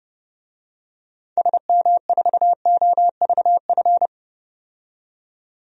Morse code, heard as a keyed tone, about 32 wpm, 700 Hz. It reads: SM4OVF
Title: DXCC-100 Call Signs - 6 Characters - 30wpm